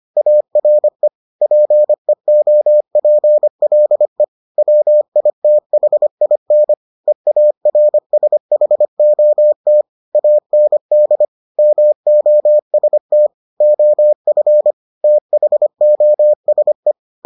Morse code, heard as a keyed tone, about 25 wpm, 600 Hz. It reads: ARE PEOPLE WITHIN EARSHOT AND MOST OF THOSE